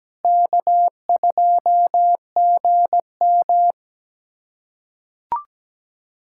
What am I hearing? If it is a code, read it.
Morse code, heard as a keyed tone, about 17 wpm, 700 Hz. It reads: K2GM
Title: USA Callsign DE Callsign 17wpm